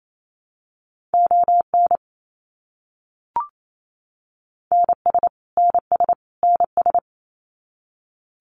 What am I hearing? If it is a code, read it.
Morse code, heard as a keyed tone, about 28 wpm, 700 Hz. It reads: ON NS NS NS
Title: Canadian Provinces and Territories - Repeated 3x - 28wpm